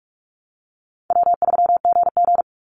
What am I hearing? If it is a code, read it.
Morse code, heard as a keyed tone, about 44 wpm, 700 Hz. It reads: W3GG
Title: Call Signs Repeated 3x - 45wpm